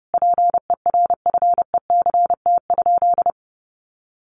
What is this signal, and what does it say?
Morse code, heard as a keyed tone, about 30 wpm, 700 Hz. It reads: PERFECT?
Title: Sentences from Top 1000 Words 30wpm